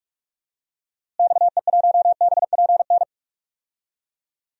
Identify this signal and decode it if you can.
Morse code, heard as a keyed tone, about 42 wpm, 690 Hz. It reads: XE1BPN
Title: DXCC-100 Call Signs - 6 Characters - Repeated 2x - 45wpm